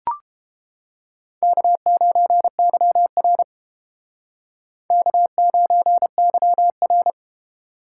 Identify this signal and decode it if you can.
Morse code, heard as a keyed tone, about 30 wpm, 690 Hz. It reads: K9YR K9YR